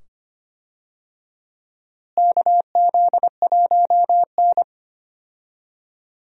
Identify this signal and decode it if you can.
Morse code, heard as a keyed tone, about 25 wpm, 700 Hz. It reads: KZ1N
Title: USA Call Signs - 4 Characters - Repeated 2x - 25wpm